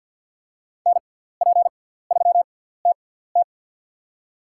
Morse code, heard as a keyed tone, about 47 wpm, spaced as for 17 wpm, 700 Hz. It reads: NP3TT
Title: USA Call Signs - 6x Character Spacing - 50wpm